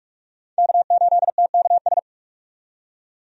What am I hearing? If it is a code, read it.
Morse code, heard as a keyed tone, about 46 wpm, 700 Hz. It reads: K8TKS
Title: USA Call Signs - 5 Characters - 45wpm